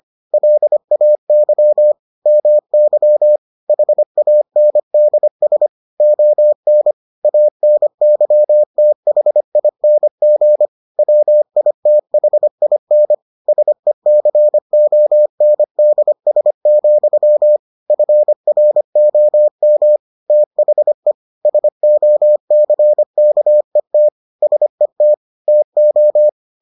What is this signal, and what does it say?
Morse code, heard as a keyed tone, about 25 wpm, 600 Hz. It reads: LAY MY HANDS ON ANYTHING WITHIN SECONDS, FROM THE SOCKET SET TO